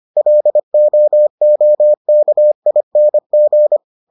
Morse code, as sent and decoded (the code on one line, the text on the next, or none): .-.. --- --- -.- .. -. --.
LOOKING